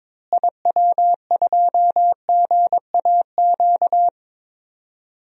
Morse code, heard as a keyed tone, about 22 wpm, 700 Hz. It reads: IW2GAQ
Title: DXCC-100 Call Signs - 6 Characters - Repeated 3x - 22wpm